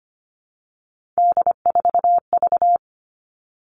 Morse code, heard as a keyed tone, about 25 wpm, 700 Hz. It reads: D4V
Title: All Cut Numbers - 3-Digit 25wpm